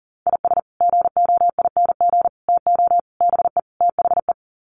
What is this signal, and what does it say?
Morse code, heard as a keyed tone, about 40 wpm, 700 Hz. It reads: IS GOING TO BE THE
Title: Sets of 5 Words 40wpm